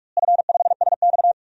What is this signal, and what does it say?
Morse code, heard as a keyed tone, about 46 wpm, 700 Hz. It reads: F5SX
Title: DXCC-100 Call Signs - 4 Characters - Repeated 3x - 45wpm